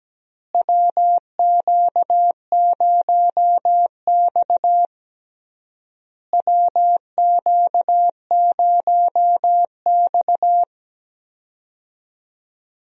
Morse code, heard as a keyed tone, about 17 wpm, 700 Hz. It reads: WQ0X WQ0X
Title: USA Call Signs - 4 Characters - Repeated 3x - 17wpm